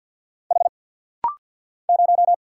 Morse code, heard as a keyed tone, about 49 wpm, 690 Hz. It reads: H 0
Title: Single Letter-Number - Rapid-Fire 50wpm